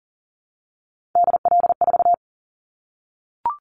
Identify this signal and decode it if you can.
Morse code, heard as a keyed tone, about 39 wpm, 700 Hz. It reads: DL4
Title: Callsign Prefixes for Top 50 DXCC - 40wpm